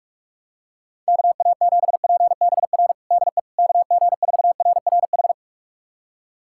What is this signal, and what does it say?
Morse code, heard as a keyed tone, about 46 wpm, 700 Hz. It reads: KA7PBR DE KG4RRH
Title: USA Callsign DE Callsign - 6 Characters - 45wpm